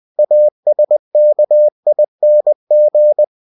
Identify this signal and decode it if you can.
Morse code, heard as a keyed tone, about 20 wpm, 600 Hz. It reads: ASKING
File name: cw-books / winnie-the-pooh / winnie-20wpm-page-0080